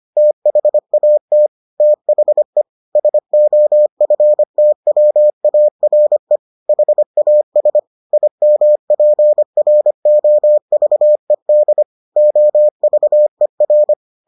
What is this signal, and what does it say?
Morse code, heard as a keyed tone, about 25 wpm, 600 Hz. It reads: THAT THE SOFTWARE HAS IMPROVED OVER